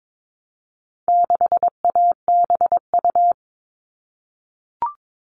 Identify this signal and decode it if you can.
Morse code, heard as a keyed tone, about 22 wpm, 700 Hz. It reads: 6ABU
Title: All Cut Numbers - 4-Digit 22wpm